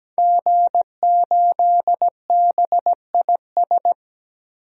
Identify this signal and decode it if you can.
Morse code, heard as a keyed tone, about 17 wpm, 700 Hz. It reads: G8BIS
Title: DXCC-100 Call Signs - 5 Characters - 17wpm